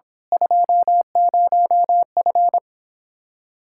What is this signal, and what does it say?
Morse code, heard as a keyed tone, about 24 wpm, 700 Hz. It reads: 20F
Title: Speed-Racing Common QSO Phrases - 22wpm